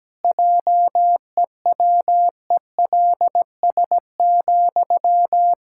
Morse code, heard as a keyed tone, about 17 wpm, 700 Hz. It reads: JEWELS,
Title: Pangram Challenge v2 - 17wpm